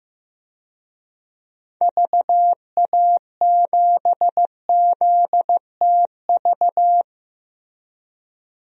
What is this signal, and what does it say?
Morse code, heard as a keyed tone, about 15 wpm, 700 Hz. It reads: VA7ZTV
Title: DXCC-100 Call Signs - 6 Characters - Repeated 2x - 15wpm